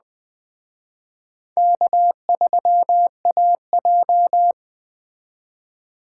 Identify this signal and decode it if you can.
Morse code, heard as a keyed tone, about 20 wpm, 700 Hz. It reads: K3AJ
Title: Speed Racing - USA 1x2 Call Signs 17wpm